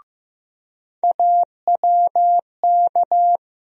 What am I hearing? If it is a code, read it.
Morse code, heard as a keyed tone, about 15 wpm, 700 Hz. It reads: AWK
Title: USA Call Sign Postfixes - 3 Letters 15wpm